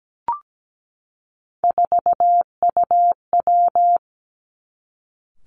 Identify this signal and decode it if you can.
Morse code, heard as a keyed tone, about 17 wpm, 700 Hz. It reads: 4UW
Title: Three Challenging Letter-Number Combinations 17wpm